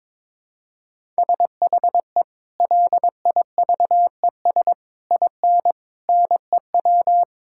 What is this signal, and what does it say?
Morse code, heard as a keyed tone, about 22 wpm, 700 Hz. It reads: SHE LIVES IN NEW
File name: practice / 22wpm / Sentences with 3-Word Phrases - 22wpm